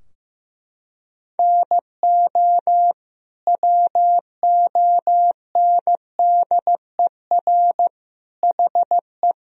Morse code, heard as a keyed tone, about 15 wpm, 700 Hz. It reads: NO WONDER HE
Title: Sentences from Top 1000 Words - Review Part 3 - 15wpm